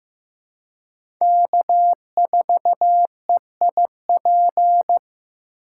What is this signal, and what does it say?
Morse code, heard as a keyed tone, about 15 wpm, 700 Hz. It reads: K4EIP
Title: USA Call Signs - 5 Characters - 4x Character Spacing - 15wpm